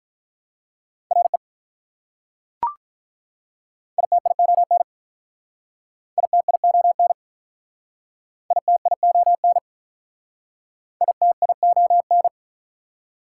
Morse code, heard as a keyed tone, about 43 wpm, 690 Hz. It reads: RE ITION ITION ITION ITION
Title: Speed-Racing Prefixes and Suffixes 35wpm